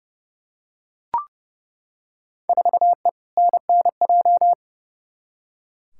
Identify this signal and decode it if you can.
Morse code, heard as a keyed tone, about 30 wpm, 700 Hz. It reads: 4E NNJ
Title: Field Day Exchanges - 30wpm